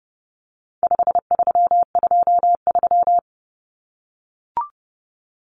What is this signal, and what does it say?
Morse code, heard as a keyed tone, about 28 wpm, 700 Hz. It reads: 5323
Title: Numbers - 4-Digit 30wpm